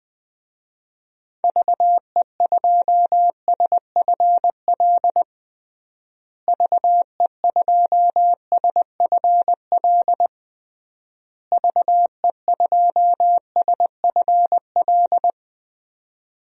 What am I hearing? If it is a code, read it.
Morse code, heard as a keyed tone, about 20 wpm, 700 Hz. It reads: VE2SFL VE2SFL VE2SFL
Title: DXCC-100 Call Signs - 6 Characters - Repeated 3x - 20wpm